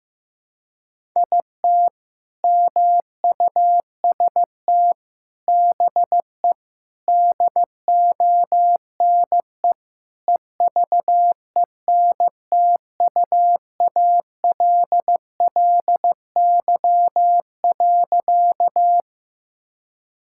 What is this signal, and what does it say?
Morse code, heard as a keyed tone, about 15 wpm, 700 Hz. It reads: IT MUST BE DONE EVENTUALLY.
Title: Sentences from Top 1000 Words - Review Part 6 - 15wpm